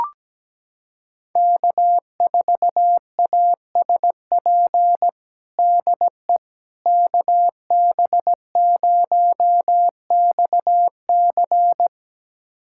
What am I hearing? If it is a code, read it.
Morse code, heard as a keyed tone, about 17 wpm, 700 Hz. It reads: K4ASP DE KB0XC
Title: USA Callsign DE Callsign - 5 Characters - 17wpm